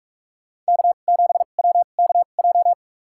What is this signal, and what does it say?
Morse code, heard as a keyed tone, about 46 wpm, spaced as for 33 wpm, 700 Hz. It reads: K7WKJ
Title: USA Call Signs - 2x Character Spacing - 45wpm